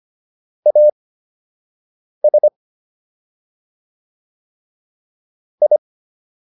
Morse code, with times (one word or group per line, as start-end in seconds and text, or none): 0.7-2.5 AS
5.6-5.8 I